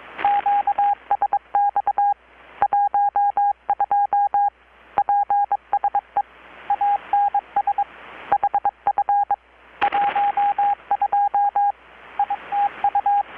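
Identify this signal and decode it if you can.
Morse code, heard as a keyed tone, about 21 wpm, 810 Hz, with radio noise. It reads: QSX 12 PSE ANS HF 12 ITU
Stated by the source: Morse from KPH's Night of Nights XX maritime radiotelegraphy event, received in Chiapas, Mexico about 3,800 km away on a Kenwood R-600 with a grounded 10 m long-wire antenna.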